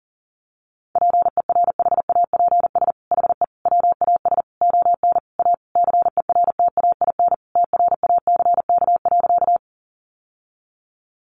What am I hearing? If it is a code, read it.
Morse code, heard as a keyed tone, about 37 wpm, 700 Hz. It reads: PERHAPS HE WAS ON A CERTAIN TRACK.
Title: Sentences from Top 1100 Words 40wpm